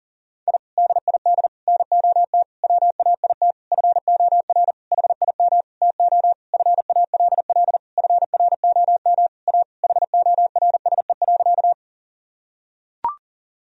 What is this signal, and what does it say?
Morse code, heard as a keyed tone, about 37 wpm, 700 Hz. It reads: I DID NOT WAIT FOR HIM TO FALL FROM A HORSE.
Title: Sentences from Top 1000 Words - Review Part 2 - 40wpm